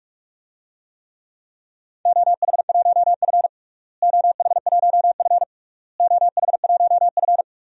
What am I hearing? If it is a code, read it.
Morse code, heard as a keyed tone, about 46 wpm, 700 Hz. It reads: OH1F OH1F OH1F
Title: DXCC-100 Call Signs - 4 Characters - Repeated 3x - 45wpm